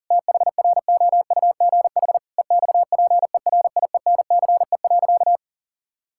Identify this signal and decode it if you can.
Morse code, heard as a keyed tone, about 40 wpm, 700 Hz. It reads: THROUGH EXPERIENCE.
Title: Sentences from Top 1000 Words - Review Part 5 - 40wpm